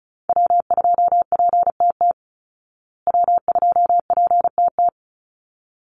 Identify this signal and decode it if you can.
Morse code, heard as a keyed tone, about 34 wpm, 700 Hz. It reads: W2PTT W2PTT
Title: Call Signs Repeated 3x - 35wpm